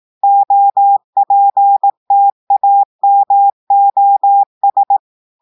Morse code, heard as a keyed tone, about 18 wpm, 800 Hz. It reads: OPTAMOS
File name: audios / portugais / portugais texte en clair 1 vitesse 18 mots minutes